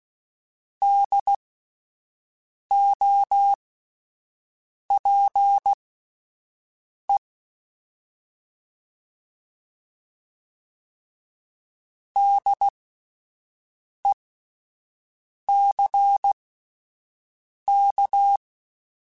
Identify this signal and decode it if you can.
Morse code, heard as a keyed tone, about 16 wpm, spaced as for 5 wpm, 800 Hz. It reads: DOPE DECK